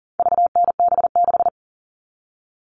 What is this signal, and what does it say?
Morse code, heard as a keyed tone, about 40 wpm, 700 Hz. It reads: VNB6
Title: All Cut Numbers - 4-Digit 40wpm